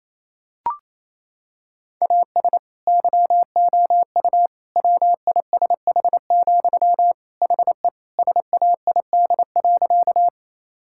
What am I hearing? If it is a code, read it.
Morse code, heard as a keyed tone, about 28 wpm, 700 Hz. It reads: AS YOU WISH, HE SAID.
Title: Sentences from Top 1600 Words 28wpm